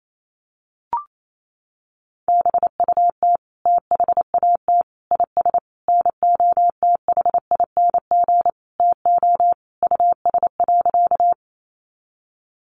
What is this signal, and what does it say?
Morse code, heard as a keyed tone, about 28 wpm, 700 Hz. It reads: BUT THAT IS NOTHING TO US.
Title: Sentences from Top 500 Words 28wpm